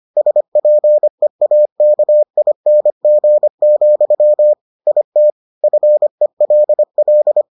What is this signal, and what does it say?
Morse code, heard as a keyed tone, about 25 wpm, 600 Hz. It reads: SPEAKING, IT FELL